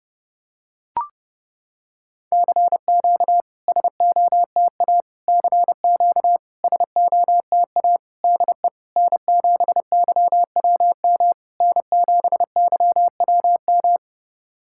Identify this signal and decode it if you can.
Morse code, heard as a keyed tone, about 30 wpm, 700 Hz. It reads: CQ SOTA CQ SOTA DE N7YWM N7YWM
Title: SOTA QSO Practice 30wpm